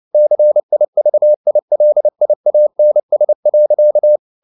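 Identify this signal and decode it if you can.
Morse code, heard as a keyed tone, about 29 wpm, 600 Hz. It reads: CIVILIANS.
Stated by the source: Morse code at 29 wpm, sending Ken Follett's novel Never in English, from CW med Gnister.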